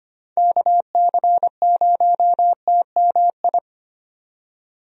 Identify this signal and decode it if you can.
Morse code, heard as a keyed tone, about 25 wpm, 700 Hz. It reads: KC0TMI
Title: USA Call Signs - 6 Characters - 25wpm